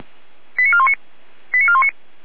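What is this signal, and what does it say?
Five-tone selective call (ZVEI), heard as quick runs of five tones, tones of about 70 ms. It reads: 89419 89419
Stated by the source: ZVEI five-tone selective call 89436, received on an SDR.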